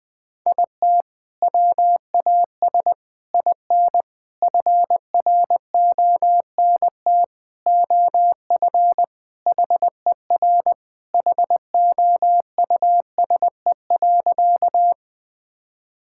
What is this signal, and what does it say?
Morse code, heard as a keyed tone, about 20 wpm, 700 Hz. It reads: IT WAS IN FRONT OF HER HOUSE.
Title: Sentences with 4-Word Phrases - 20wpm